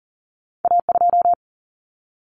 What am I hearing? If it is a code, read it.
Morse code, heard as a keyed tone, about 38 wpm, 700 Hz. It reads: A2
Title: Common Cut Numbers - 2-Digit 40wpm